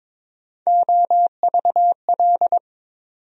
Morse code, heard as a keyed tone, about 22 wpm, 700 Hz. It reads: OVL
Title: USA Call Sign Postfixes - 3 Letters 22wpm